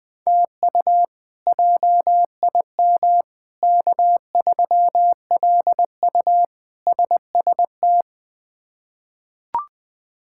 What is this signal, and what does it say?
Morse code, heard as a keyed tone, about 20 wpm, 700 Hz. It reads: TU JIM K3LU SST
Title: K1USN SST QSO Practice - 20wpm